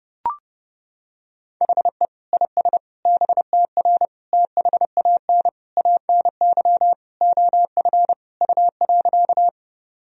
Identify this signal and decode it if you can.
Morse code, heard as a keyed tone, about 30 wpm, 700 Hz. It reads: HE IS BTR THAN ANY OF U.
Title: Sentences from Top 400 Words and Abbreviations 30wpm